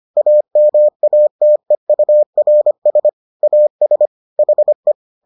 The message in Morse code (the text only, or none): AMATEURS AS HE